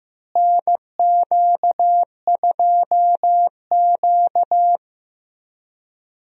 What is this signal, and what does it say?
Morse code, heard as a keyed tone, about 15 wpm, 700 Hz. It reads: NQ2Q
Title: USA Call Signs - 4 Characters - Repeated 3x - 15wpm